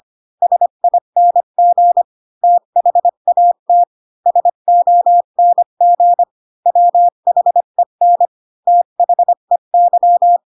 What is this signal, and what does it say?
Morse code, heard as a keyed tone, about 25 wpm, 700 Hz. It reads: SING THAT SONG WHEN THEY
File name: Pinocchio / Pinocchio ch8 20 25 ik1hge